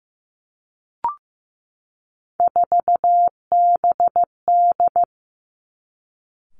Morse code, heard as a keyed tone, about 15 wpm, 700 Hz. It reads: 4BD
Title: Three Challenging Letter-Number Combinations 15wpm